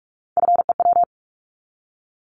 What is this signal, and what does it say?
Morse code, heard as a keyed tone, about 46 wpm, 700 Hz. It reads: FEW
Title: Multi-letter Phonemes 45wpm